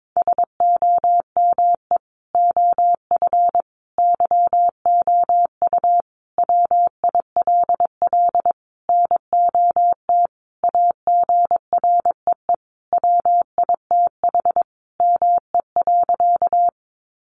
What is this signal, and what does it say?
Morse code, heard as a keyed tone, about 22 wpm, 700 Hz. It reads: SOME OF YOU WILL NOT AGREE WITH ME.
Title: Sentences from Top 1100 Words 22wpm